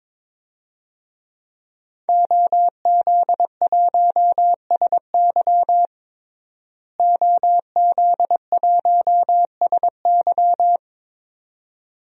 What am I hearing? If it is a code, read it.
Morse code, heard as a keyed tone, about 22 wpm, 700 Hz. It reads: OZ1SY OZ1SY
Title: DXCC-100 Call Signs - 5 Characters - Repeated 2x - 22wpm